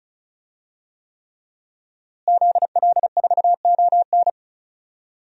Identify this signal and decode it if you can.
Morse code, heard as a keyed tone, about 36 wpm, 700 Hz. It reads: ZL4ON
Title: DXCC-100 Call Signs - 5 Characters - Repeated 3x - 35wpm